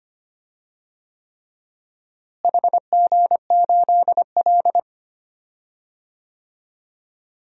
Morse code, heard as a keyed tone, about 25 wpm, 700 Hz. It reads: HG8L
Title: DXCC-100 Call Signs - 4 Characters - Repeated 3x - 25wpm